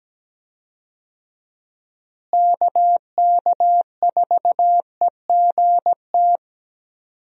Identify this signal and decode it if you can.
Morse code, heard as a keyed tone, about 17 wpm, 700 Hz. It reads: KK4EGT
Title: USA Call Signs - 6 Characters - Repeated 3x - 17wpm